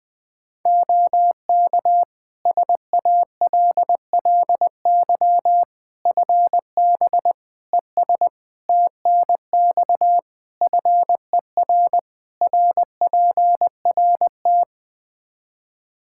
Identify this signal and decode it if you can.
Morse code, heard as a keyed tone, about 20 wpm, 700 Hz. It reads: OK SALLY FB ES TNX FER RPRT
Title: Practice QSO - Exchange 2 - Info - 20wpm